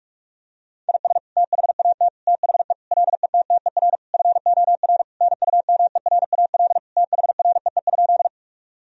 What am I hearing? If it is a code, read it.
Morse code, heard as a keyed tone, about 46 wpm, 700 Hz. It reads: IS THAT THE LETTER FOR NUMERAL THREE?
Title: Sentences from Top 1000 Words - Review Part 2 - 45wpm